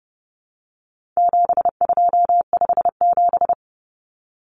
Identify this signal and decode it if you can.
Morse code, heard as a keyed tone, about 30 wpm, 700 Hz. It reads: 7257
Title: Numbers - 4-Digit 30wpm